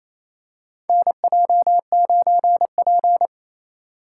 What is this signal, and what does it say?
Morse code, heard as a keyed tone, about 28 wpm, 700 Hz. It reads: NJ9P
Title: USA Call Signs - 4 Characters - 3x Character Spacing - 28wpm